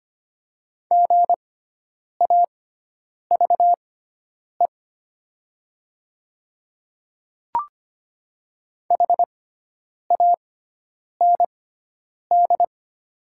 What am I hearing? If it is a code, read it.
Morse code, heard as a keyed tone, about 25 wpm, spaced as for 9 wpm, 700 Hz. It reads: GAVE HAND
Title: 6x Character Spacing - Top 500 Words - 25wpm